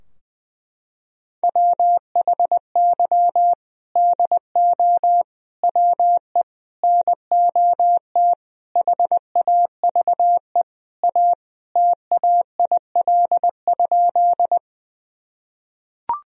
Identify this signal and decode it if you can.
Morse code, heard as a keyed tone, about 21 wpm, 700 Hz. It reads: WHY DO WE NOT HAVE A TAIL?
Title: Sentences from Top 1000 Words - Review Part 3 - 20wpm